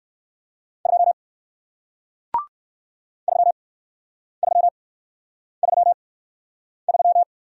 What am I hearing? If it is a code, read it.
Morse code, heard as a keyed tone, about 58 wpm, spaced as for 9 wpm, 700 Hz. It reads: 2 3333